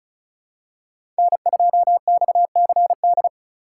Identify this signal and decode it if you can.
Morse code, heard as a keyed tone, about 34 wpm, 700 Hz. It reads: N2XCD
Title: USA Call Signs - 5 Characters - 4x Character Spacing - 35wpm